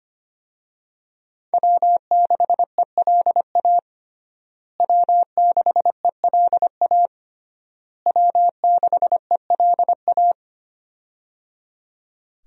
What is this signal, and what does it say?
Morse code, heard as a keyed tone, about 25 wpm, 700 Hz. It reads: W6ELA W6ELA W6ELA
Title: USA Call Signs - 5 Characters - Repeated 3x - 25wpm